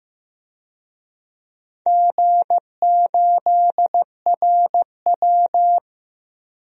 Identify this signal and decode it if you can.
Morse code, heard as a keyed tone, about 16 wpm, 700 Hz. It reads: G8RW
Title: DXCC-100 Call Signs - 4 Characters - 15wpm